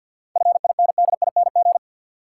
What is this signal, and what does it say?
Morse code, heard as a keyed tone, about 53 wpm, 700 Hz. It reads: FINDING
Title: Sentences from Top 1100 Words - New Word Drill Alt - 50wpm